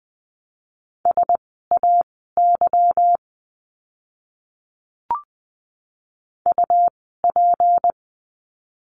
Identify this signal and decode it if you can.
Morse code, heard as a keyed tone, about 20 wpm, spaced as for 15 wpm, 700 Hz. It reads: SAY UP
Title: 160 Lesson 26f - Letter g - Words 20wpm